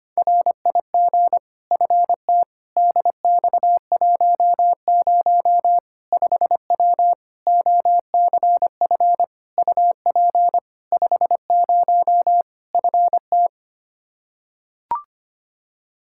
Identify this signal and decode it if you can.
Morse code, heard as a keyed tone, about 25 wpm, 700 Hz. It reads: RIG FT DX10 5W OCF UP 50 FT
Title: Practice QSO - Exchange 2 - Info - 25wpm